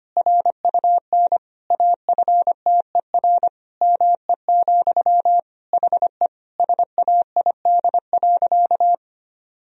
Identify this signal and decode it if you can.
Morse code, heard as a keyed tone, about 25 wpm, 700 Hz. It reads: RUN AFTER ME, HE SAID.
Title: Sentences from Top 1000 Words - Review Part 1 - 25wpm